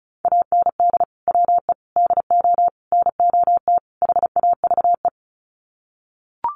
Morse code, heard as a keyed tone, about 34 wpm, 700 Hz. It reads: AND WE DO NOT HAVE
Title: Sets of 5 Words Encore 35wpm